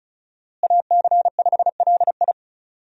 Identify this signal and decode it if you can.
Morse code, heard as a keyed tone, about 36 wpm, 700 Hz. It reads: AC5LI
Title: USA Call Signs - 5 Characters - Repeated 2x - 35wpm